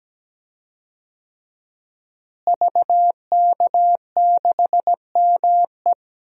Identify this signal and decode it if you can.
Morse code, heard as a keyed tone, about 17 wpm, 700 Hz. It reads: VK6ME
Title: DXCC-100 Call Signs - 5 Characters - Repeated 2x - 17wpm